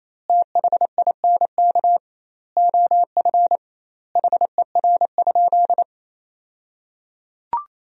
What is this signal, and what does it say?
Morse code, heard as a keyed tone, about 28 wpm, 700 Hz. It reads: THINK OF HER?
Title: 2x Word-Spacing - Sentences from Top 200 Words - 28wpm